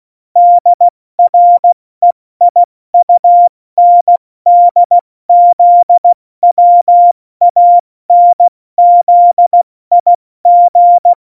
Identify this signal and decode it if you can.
Morse code, heard as a keyed tone, about 16 wpm, 700 Hz. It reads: DREIUNDZWANZIG
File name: Morsekurs / Witze / Witz 27. Februar 2018, 70 F80